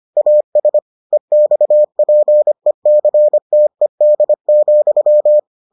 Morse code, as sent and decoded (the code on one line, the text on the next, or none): .- ... / . -..- .--. -.-. - . -.. --..--
AS EXPECTED,